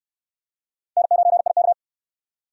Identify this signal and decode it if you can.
Morse code, heard as a keyed tone, about 63 wpm, 700 Hz. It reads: N9IK